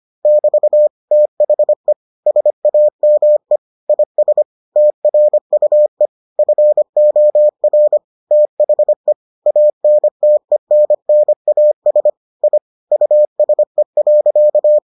Morse code, heard as a keyed tone, about 25 wpm, 600 Hz. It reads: <BT> THE SAME IS TRUE FOR THE ANTENNAS I USE.